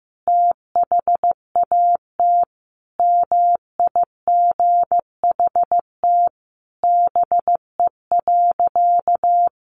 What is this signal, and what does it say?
Morse code, heard as a keyed tone, about 15 wpm, 700 Hz. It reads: THAT MIGHT BE.
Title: Sentences from Top 1500 Words 15wpm